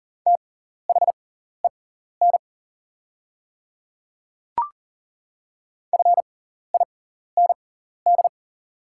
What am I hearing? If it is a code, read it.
Morse code, heard as a keyed tone, about 40 wpm, spaced as for 14 wpm, 690 Hz. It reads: THEN FIND
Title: 6x Character Spacing - Top 300 Words - 40wpm